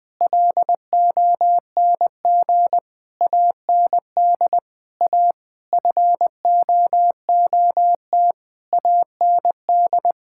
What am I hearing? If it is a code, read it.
Morse code, heard as a keyed tone, about 20 wpm, 700 Hz. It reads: LONG AND A FOOT AND
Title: Sentences from Top 1000 Words - Review Part 6 - 20wpm